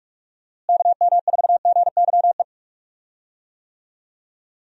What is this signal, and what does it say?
Morse code, heard as a keyed tone, about 44 wpm, 700 Hz. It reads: KM4GYE
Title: USA Call Signs - 6 Characters - Repeated 3x - 45wpm